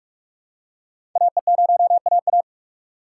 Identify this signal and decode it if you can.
Morse code, heard as a keyed tone, about 45 wpm, 700 Hz. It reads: AE0AA
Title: USA Call Signs - 5 Characters - 3x Character Spacing - 45wpm